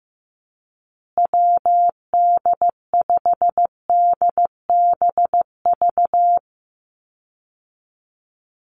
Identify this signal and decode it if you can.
Morse code, heard as a keyed tone, about 15 wpm, 700 Hz. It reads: WD5DBV
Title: Call Signs Repeated 2x - 15wpm